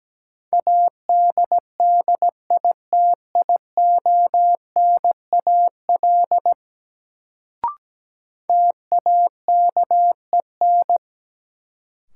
Morse code, heard as a keyed tone, about 17 wpm, 700 Hz. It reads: ADDITIONAL TAKEN
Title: Sentences from Top 1000 Words - Review Part 4 - New Word Drill - 17wpm